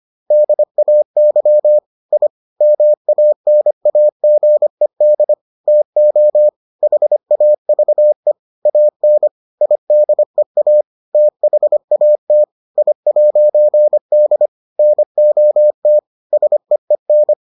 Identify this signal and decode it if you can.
Morse code, heard as a keyed tone, about 25 wpm, 600 Hz. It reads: DAY I MANAGED TO HAVE AN IDEA THAT I'D NOT SEEN